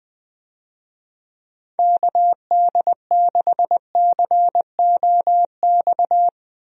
Morse code, heard as a keyed tone, about 21 wpm, 700 Hz. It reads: KD6COX
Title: USA Call Signs - 6 Characters - Repeated 2x - 20wpm